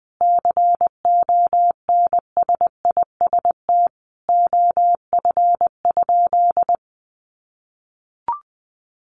Morse code, heard as a keyed tone, about 20 wpm, 700 Hz. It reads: CONSIST OF?
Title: Sentences from Top 1400 Words 20wpm